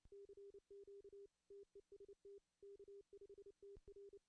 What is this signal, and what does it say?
Morse code, heard as a keyed tone, about 29 wpm, 400 Hz. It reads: CQ TEST K5TR
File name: CW / cqww-cw-k5tr